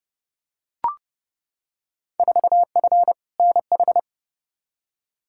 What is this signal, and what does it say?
Morse code, heard as a keyed tone, about 30 wpm, 700 Hz. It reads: 4F NH
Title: Field Day Exchanges Encore - 30wpm